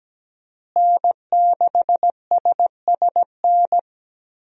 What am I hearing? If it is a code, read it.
Morse code, heard as a keyed tone, about 17 wpm, 700 Hz. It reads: N6SSN
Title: USA Call Signs - 5 Characters - Repeated 3x - 17wpm